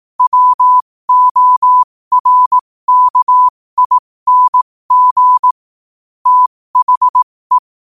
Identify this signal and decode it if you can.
Morse code, heard as a keyed tone, about 18 wpm, spaced as for 15 wpm, 1000 Hz. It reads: WORKING THE
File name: qotdfiles / 15wpm